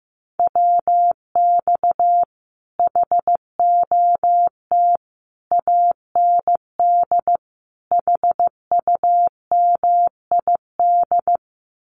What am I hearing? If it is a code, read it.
Morse code, heard as a keyed tone, about 15 wpm, 700 Hz. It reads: WX HOT AND HUMID
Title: Common QSO Phrases Encore 15wpm